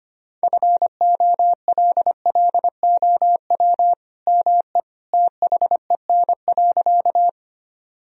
Morse code, heard as a keyed tone, about 25 wpm, 700 Hz. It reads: FOLLOW ME THEN.
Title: Sentences from Top 1000 Words - Review Part 1 - 25wpm